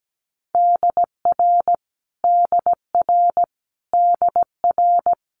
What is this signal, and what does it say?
Morse code, heard as a keyed tone, about 17 wpm, 700 Hz. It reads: DR DR DR
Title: Multi-letter Phonemes Repeated 3x 17wpm